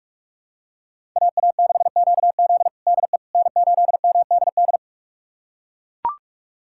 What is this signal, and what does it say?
Morse code, heard as a keyed tone, about 41 wpm, 700 Hz. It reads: AA6QZ DE N8MDD
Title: USA Callsign DE Callsign - 5 Characters - 45wpm